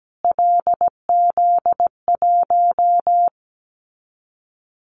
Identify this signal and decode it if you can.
Morse code, heard as a keyed tone, about 17 wpm, 700 Hz. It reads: LZ1
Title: Callsign Prefixes for Top 25 DXCC - Repeated 3x - 17wpm